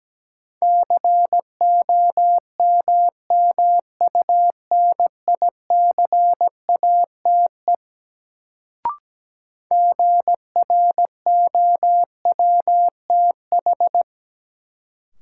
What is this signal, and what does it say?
Morse code, heard as a keyed tone, about 17 wpm, 700 Hz. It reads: COMMUNICATE GROWTH
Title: Sentences from Top 1300 Words - New Word Drill Alt - 17wpm